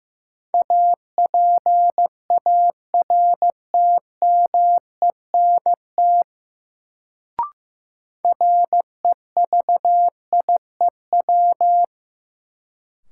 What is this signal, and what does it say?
Morse code, heard as a keyed tone, about 15 wpm, 700 Hz. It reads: APARTMENT REVIEW
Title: Sentences from Top 900 Words - New Word Drill - 15wpm